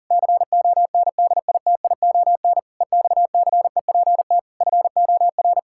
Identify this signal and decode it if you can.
Morse code, heard as a keyed tone, about 40 wpm, 700 Hz. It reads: CONDITION EXCEPT FOR
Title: Pangram Challenge v2 - 40wpm